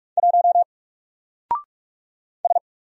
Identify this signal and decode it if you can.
Morse code, heard as a keyed tone, about 46 wpm, 700 Hz. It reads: 1 S
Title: Single Character (Letter:Number:Punctuation:Prosign) 45wpm